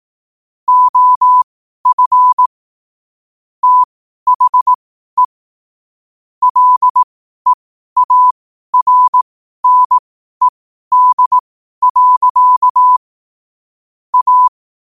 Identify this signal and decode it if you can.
Morse code, heard as a keyed tone, about 18 wpm, spaced as for 12 wpm, 1000 Hz. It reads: OF THE LEARNED. A
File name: qotdfiles / 12wpm